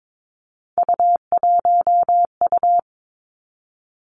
Morse code, heard as a keyed tone, about 22 wpm, 700 Hz. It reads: U1U
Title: Three Challenging Letter-Number Combinations 22wpm